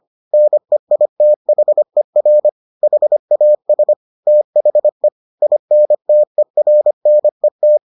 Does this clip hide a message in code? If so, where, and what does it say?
Yes: Morse code from 0.3 to 7.8 s, heard as a keyed tone, about 25 wpm, 600 Hz. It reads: NEITHER HAS THE INTERNET